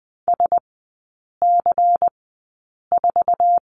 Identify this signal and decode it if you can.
Morse code, heard as a keyed tone, about 20 wpm, spaced as for 8 wpm, 700 Hz. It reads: SC4
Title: 121 Lesson 20e - Number 3 - Three Characters 20wpm